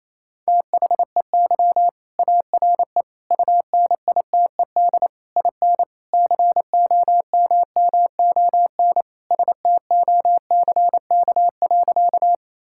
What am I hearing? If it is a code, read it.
Morse code, heard as a keyed tone, about 28 wpm, 700 Hz. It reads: THEY ARE UNITED IN COMMON STOCK.
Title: Sentences from Top 1000 Words - Review Part 5 - 28wpm